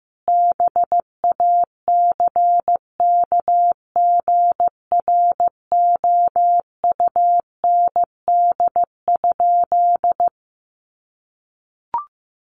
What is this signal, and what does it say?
Morse code, heard as a keyed tone, about 15 wpm, 700 Hz. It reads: BACKGROUND?
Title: Sentences from Top 1300 Words 15wpm